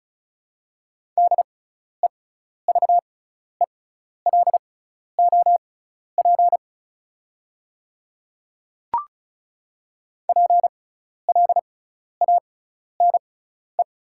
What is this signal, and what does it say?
Morse code, heard as a keyed tone, about 34 wpm, spaced as for 12 wpm, 690 Hz. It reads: DEVELOP PLANE
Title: 6x Character Spacing - Top 500 Words - 35wpm